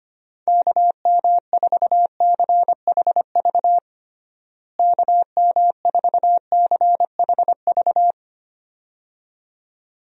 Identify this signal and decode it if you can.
Morse code, heard as a keyed tone, about 25 wpm, 700 Hz. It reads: KM4CHV KM4CHV
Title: USA Call Signs - 6 Characters - Repeated 2x - 25wpm